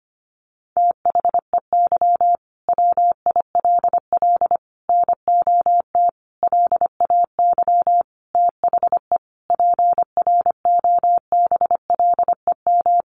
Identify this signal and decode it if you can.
Morse code, heard as a keyed tone, about 25 wpm, 700 Hz. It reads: THEY WILL NOT LAY THE PROBLEM
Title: Sentences from Top 1500 Words 25wpm